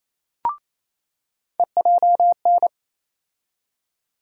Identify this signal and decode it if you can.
Morse code, heard as a keyed tone, about 28 wpm, 700 Hz. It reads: EJN
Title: USA Call Sign Postfixes - 3 Letters 28wpm